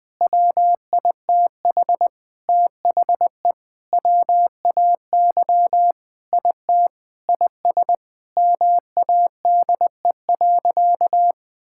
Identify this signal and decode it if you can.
Morse code, heard as a keyed tone, about 20 wpm, 700 Hz. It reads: WITH THE WAY IT IS MADE.
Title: Sentences with 5-Word Phrases - 20wpm